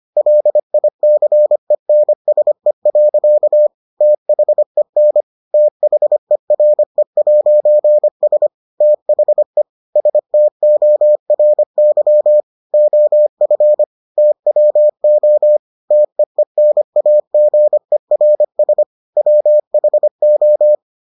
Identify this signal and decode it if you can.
Morse code, heard as a keyed tone, about 25 wpm, 600 Hz. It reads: LICENSE. THEN THERE'S THE STORY OF TWO TEENAGERS WHO